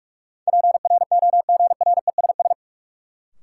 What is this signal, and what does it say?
Morse code, heard as a keyed tone, about 46 wpm, 700 Hz. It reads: PROGRESS
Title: Sentences from Top 1400 Words - New Word Drill - 45wpm